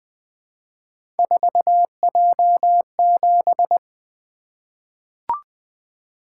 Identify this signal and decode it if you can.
Morse code, heard as a keyed tone, about 21 wpm, 700 Hz. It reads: 4J7
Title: Three Letter-Number Combinations - 20wpm